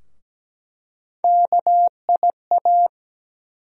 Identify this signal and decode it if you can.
Morse code, heard as a keyed tone, about 17 wpm, 700 Hz. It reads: KIA
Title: USA Call Sign Postfixes - 3 Letters 17wpm